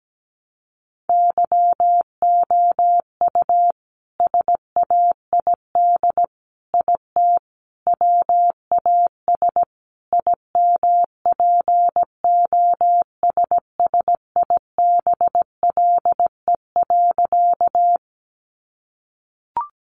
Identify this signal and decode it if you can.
Morse code, heard as a keyed tone, about 17 wpm, 700 Hz. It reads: YOU SAID IT WAS IMPOSSIBLE.
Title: Sentences from Top 1100 Words 17wpm